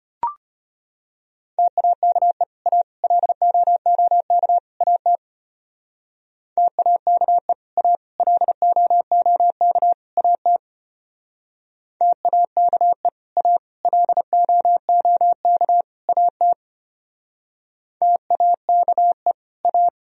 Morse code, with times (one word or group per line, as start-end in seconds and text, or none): 1.6-2.4 TAKE
2.7-2.8 A
3.0-4.6 LOOK
4.8-5.2 AT
6.6-7.5 TAKE
7.8-8.0 A
8.2-9.9 LOOK
10.2-10.6 AT
12.0-13.1 TAKE
13.4-13.6 A
13.9-15.8 LOOK
16.1-16.5 AT
18.0-19.3 TAKE
19.6-19.9 A